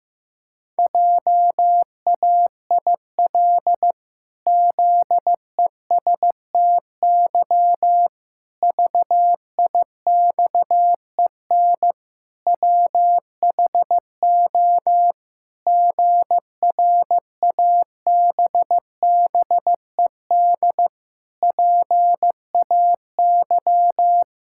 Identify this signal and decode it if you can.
Morse code, heard as a keyed tone, about 16 wpm, 700 Hz. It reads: JAIL ZESTY VIXEN WHO GRABBED PAY
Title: Pangram Challenge v2 - 15wpm